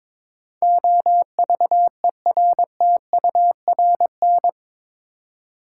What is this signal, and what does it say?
Morse code, heard as a keyed tone, about 21 wpm, 700 Hz. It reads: OVERTURN
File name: practice / 22wpm / Eight Letter Words - 22wpm